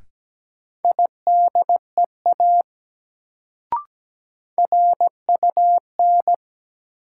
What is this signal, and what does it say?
Morse code, heard as a keyed tone, about 17 wpm, 700 Hz. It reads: IDEA RUN
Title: Sentences from Top 300 Words - New Word Drill - 17wpm